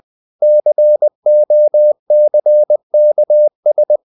Morse code, heard as a keyed tone, about 20 wpm, 600 Hz. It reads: COCKS
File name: the old man and the sea / the old man and the sea part 1 20wpm